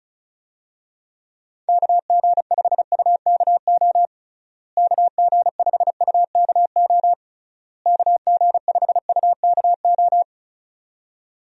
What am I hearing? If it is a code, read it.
Morse code, heard as a keyed tone, about 36 wpm, 700 Hz. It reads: KG5UKO KG5UKO KG5UKO
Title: USA Call Signs - 6 Characters - Repeated 3x - 35wpm